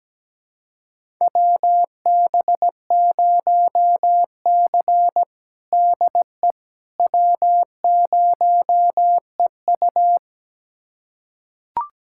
Morse code, heard as a keyed tone, about 17 wpm, 700 Hz. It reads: WB0C DE W0EU
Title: USA Callsign DE Callsign - 4 Characters - 17wpm